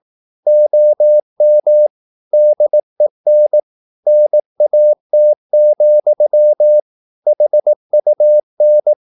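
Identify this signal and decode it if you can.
Morse code, heard as a keyed tone, about 18 wpm, 600 Hz. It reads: OM DEN NAT, HUN
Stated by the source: Slow (QRS) Morse practice at 18 wpm, CW med Gnister.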